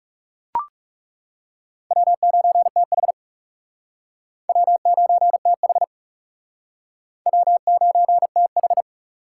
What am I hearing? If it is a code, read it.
Morse code, heard as a keyed tone, about 40 wpm, 700 Hz. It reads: W9TH W9TH W9TH